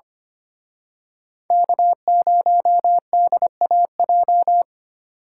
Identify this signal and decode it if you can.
Morse code, heard as a keyed tone, about 28 wpm, 700 Hz. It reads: K0DAJ
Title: Speed Racing - USA 1x3 Call Signs 25wpm